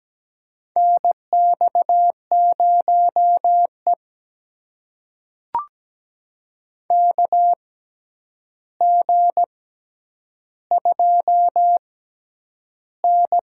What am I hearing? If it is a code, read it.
Morse code, heard as a keyed tone, about 17 wpm, spaced as for 9 wpm, 700 Hz. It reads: NX0E KG2N
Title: USA Call Signs - 4 Characters - 6x Character Spacing - 17wpm